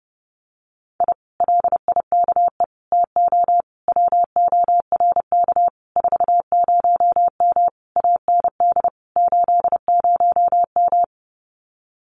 Morse code, heard as a keyed tone, about 30 wpm, 700 Hz. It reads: I LIKE TO WORK 40M AND 80M
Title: Common QSO Phrases Encore 30wpm